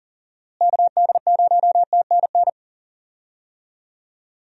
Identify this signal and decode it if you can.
Morse code, heard as a keyed tone, about 40 wpm, 700 Hz. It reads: KD0TNN
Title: USA Call Signs - 6 Characters - Repeated 3x - 40wpm